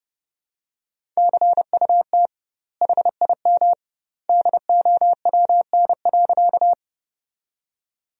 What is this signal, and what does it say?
Morse code, heard as a keyed tone, about 30 wpm, 700 Hz. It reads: CUT HIM DOWN.
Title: 2x Word-Spacing - Sentences from Top 300 Words - 30wpm